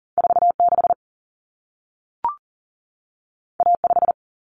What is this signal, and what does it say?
Morse code, heard as a keyed tone, about 40 wpm, 700 Hz. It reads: 4 6 A 5
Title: Common Cut Numbers - 2-Digit 40wpm